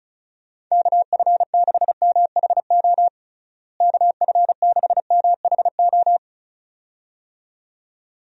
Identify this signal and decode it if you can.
Morse code, heard as a keyed tone, about 36 wpm, 700 Hz. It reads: KF6MHO KF6MHO
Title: USA Call Signs - 6 Characters - Repeated 2x - 35wpm